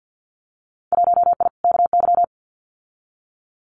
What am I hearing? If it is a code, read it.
Morse code, heard as a keyed tone, about 50 wpm, 700 Hz. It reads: 1I KY
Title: Winter Field Day Exchanges 50wpm